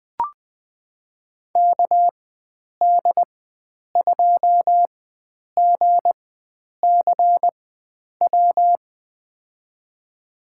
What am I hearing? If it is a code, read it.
Morse code, heard as a keyed tone, about 20 wpm, spaced as for 9 wpm, 700 Hz. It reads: KD2GCW
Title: USA Call Signs - 6 Characters - 4x Character Spacing - 20wpm